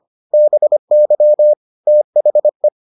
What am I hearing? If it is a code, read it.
Morse code, heard as a keyed tone, about 25 wpm, 600 Hz. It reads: BY THE